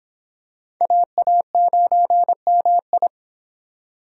Morse code, heard as a keyed tone, about 25 wpm, 700 Hz. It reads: AA9MI